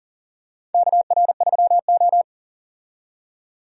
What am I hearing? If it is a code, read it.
Morse code, heard as a keyed tone, about 40 wpm, 700 Hz. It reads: KR3O
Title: USA Call Signs - 4 Characters - 40wpm